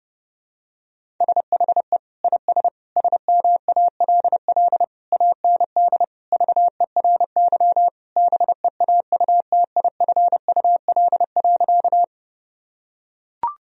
Morse code, heard as a keyed tone, about 30 wpm, 700 Hz. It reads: SHE IS SMALL AND VERY BEAUTIFUL.
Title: Sentences from Top 1000 Words - Review Part 1 - 30wpm